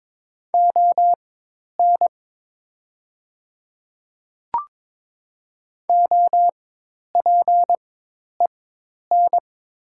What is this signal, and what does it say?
Morse code, heard as a keyed tone, about 21 wpm, spaced as for 10 wpm, 700 Hz. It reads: ON OPEN
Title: 4x Character Spacing - Top 500 Words - 22wpm